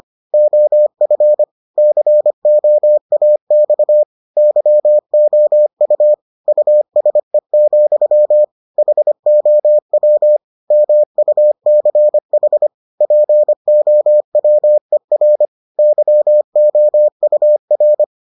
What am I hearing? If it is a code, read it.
Morse code, heard as a keyed tone, about 25 wpm, 600 Hz. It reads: OF COAX YOU USE, HOW MUCH POWER YOUR